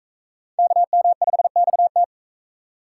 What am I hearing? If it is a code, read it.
Morse code, heard as a keyed tone, about 43 wpm, 700 Hz. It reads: KM5XT